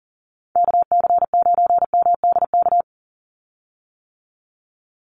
Morse code, heard as a keyed tone, about 40 wpm, 700 Hz. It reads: KC9MDK
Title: USA Call Signs 40wpm